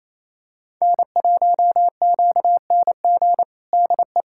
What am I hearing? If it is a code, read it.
Morse code, heard as a keyed tone, about 28 wpm, 700 Hz. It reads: N1QNG DE
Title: USA Callsign DE Callsign - 5 Characters - 28wpm